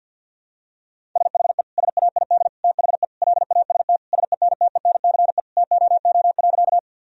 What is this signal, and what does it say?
Morse code, heard as a keyed tone, about 51 wpm, 700 Hz. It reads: SHE SAID THE LAST SENTENCE TOO.
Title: Sentences from Top 1000 Words - Review Part 1 - 50wpm